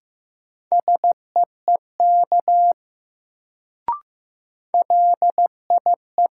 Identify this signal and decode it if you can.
Morse code, heard as a keyed tone, about 15 wpm, 700 Hz. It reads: SEEK LIE
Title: Sentences from Top 1600 Words - New Word Drill - 15wpm